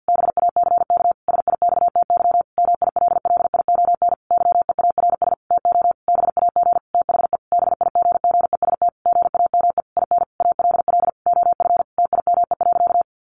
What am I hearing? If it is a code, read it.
Morse code, heard as a keyed tone, about 50 wpm, 700 Hz. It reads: BACK SIXTY MILLION YEARS TO BAG THE BIGGEST GAME IN ALL OF TIME.
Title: A Sound Of Thunder 50wpm